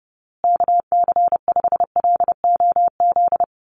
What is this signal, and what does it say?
Morse code, heard as a keyed tone, about 30 wpm, 700 Hz. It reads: KC5LOZ
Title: USA Call Signs Repeated 2x 30wpm